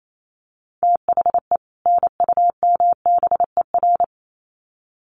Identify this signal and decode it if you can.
Morse code, heard as a keyed tone, about 28 wpm, 700 Hz. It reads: THE NUMBER
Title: Sets of 2 Words 28wpm